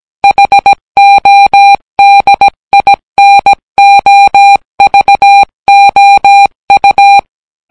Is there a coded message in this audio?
Yes: Morse code from 0.2 to 7.2 s, heard as a keyed tone, about 17 wpm, 800 Hz. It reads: HODINOVOU